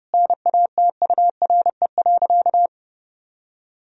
Morse code, heard as a keyed tone, about 32 wpm, 700 Hz. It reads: NATURE.
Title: Sentences from Top 1000 Words - Review Part 3 - 30wpm